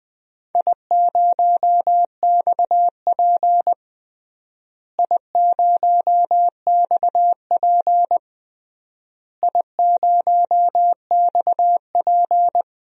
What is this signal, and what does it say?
Morse code, heard as a keyed tone, about 20 wpm, 700 Hz. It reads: I0XP I0XP I0XP
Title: DXCC-100 Call Signs - 4 Characters - Repeated 3x - 20wpm